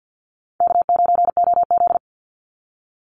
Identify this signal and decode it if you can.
Morse code, heard as a keyed tone, about 49 wpm, 700 Hz. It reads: K9OZ
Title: NA SOTA Chaser Callsigns 50wpm